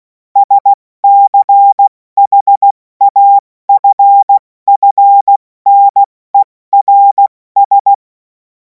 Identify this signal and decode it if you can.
Morse code, heard as a keyed tone, about 16 wpm, 800 Hz. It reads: SCHAFFNERS
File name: Morsekurs / Witze / Witz 03. Februar 2026 70 F80